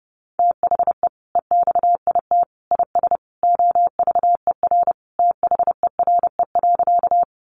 Morse code, heard as a keyed tone, about 28 wpm, 700 Hz. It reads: THE EXIT IS OVER THERE.
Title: Sentences from Top 1100 Words 30wpm